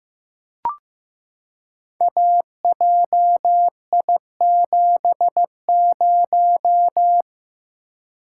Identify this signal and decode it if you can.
Morse code, heard as a keyed tone, about 15 wpm, 700 Hz. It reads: AJI70
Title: Code Groups - 5 Letter:Numbers - 15wpm